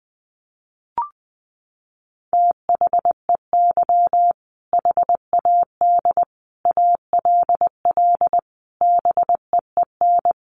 Morse code, heard as a keyed tone, about 20 wpm, 700 Hz. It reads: THEY HAD ALL BEEN
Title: Sentences from Top 200 Words 20wpm